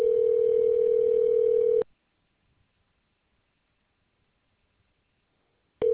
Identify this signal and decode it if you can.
Ringback tone (North America), 440 + 480 Hz.